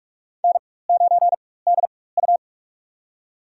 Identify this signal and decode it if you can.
Morse code, heard as a keyed tone, about 47 wpm, spaced as for 21 wpm, 700 Hz. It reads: N9DU
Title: USA Call Signs - 4 Characters - 4x Character Spacing - 45wpm